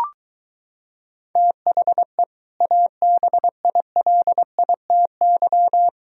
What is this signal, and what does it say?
Morse code, heard as a keyed tone, about 23 wpm, 700 Hz. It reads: THE ABILITY